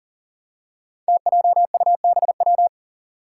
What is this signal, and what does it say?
Morse code, heard as a keyed tone, about 40 wpm, 700 Hz. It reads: TJUBW
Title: Five Challenging Letter-Number Combinations 40wpm